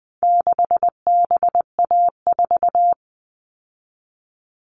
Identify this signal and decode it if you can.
Morse code, heard as a keyed tone, about 20 wpm, 700 Hz. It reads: 6BA4
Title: All Cut Numbers - 4-Digit 20wpm